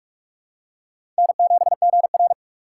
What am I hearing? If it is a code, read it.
Morse code, heard as a keyed tone, about 44 wpm, 700 Hz. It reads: N7GR
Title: USA Call Signs - 3x Character Spacing - 45wpm